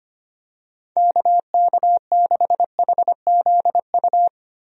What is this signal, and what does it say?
Morse code, heard as a keyed tone, about 25 wpm, 700 Hz. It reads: KK6HZU
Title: USA Call Signs - 6 Characters - Repeated 3x - 25wpm